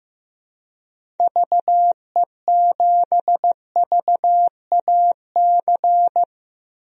Morse code, heard as a keyed tone, about 15 wpm, 700 Hz. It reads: VE7VAC
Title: DXCC-100 Call Signs - 6 Characters - 15wpm